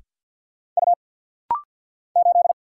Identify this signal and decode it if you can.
Morse code, heard as a keyed tone, about 50 wpm, 700 Hz. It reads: U 8
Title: Single Letter-Number - Rapid-Fire 50wpm